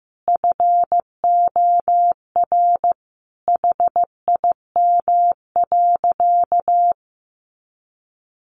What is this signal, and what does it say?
Morse code, heard as a keyed tone, about 15 wpm, 700 Hz. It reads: FOR HIM.
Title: Sentences from Top 500 Words 15wpm